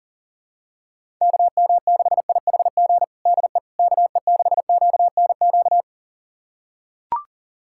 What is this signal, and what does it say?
Morse code, heard as a keyed tone, about 40 wpm, 700 Hz. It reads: KM6IHG DE KE6QNQ
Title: USA Callsign DE Callsign - 6 Characters - 40wpm